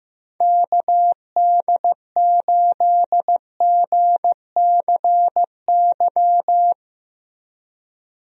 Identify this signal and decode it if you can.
Morse code, heard as a keyed tone, about 15 wpm, 700 Hz. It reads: KD8GCY
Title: USA Call Signs - 6 Characters - Repeated 3x - 15wpm